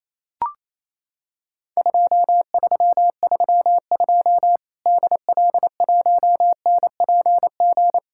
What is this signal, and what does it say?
Morse code, heard as a keyed tone, about 27 wpm, 700 Hz. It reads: 2332 DL1NPG
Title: Worked All Europe DX Contest - QTC Exchange - 28wpm